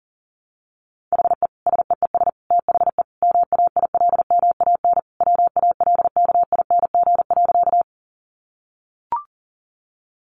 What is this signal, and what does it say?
Morse code, heard as a keyed tone, about 37 wpm, 700 Hz. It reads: HE SEES THE MAILMAN WALKING.
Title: Easy Phrases 40wpm